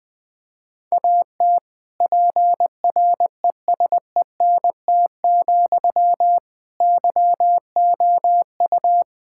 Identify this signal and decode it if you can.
Morse code, heard as a keyed tone, about 20 wpm, 700 Hz. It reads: AT PRESENT, YOU
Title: Sentences from Top 1000 Words - Review Part 5 - 20wpm